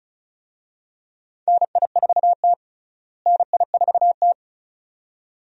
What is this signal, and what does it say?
Morse code, heard as a keyed tone, about 36 wpm, 700 Hz. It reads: NI4T NI4T
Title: USA Call Signs - 4 Characters - Repeated 2x - 35wpm